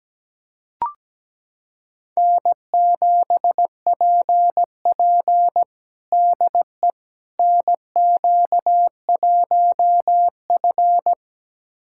Morse code, heard as a keyed tone, about 17 wpm, 700 Hz. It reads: N7PP DE NQ1F
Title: USA Callsign DE Callsign - 4 Characters - 17wpm